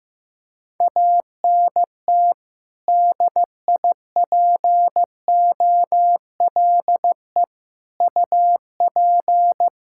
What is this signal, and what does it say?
Morse code, heard as a keyed tone, about 15 wpm, 700 Hz. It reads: ANT DIPOLE UP
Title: Practice QSO - Exchange 2 - Info - 15wpm